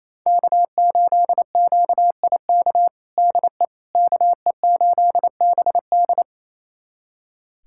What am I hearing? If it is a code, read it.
Morse code, heard as a keyed tone, about 28 wpm, 700 Hz. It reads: K8QIK DE KE8BD
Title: USA Callsign DE Callsign - 5 Characters - 28wpm